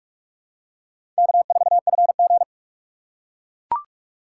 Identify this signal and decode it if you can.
Morse code, heard as a keyed tone, about 42 wpm, 700 Hz. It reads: K4FG
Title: USA Call Signs - 4 Characters - 3x Character Spacing - 45wpm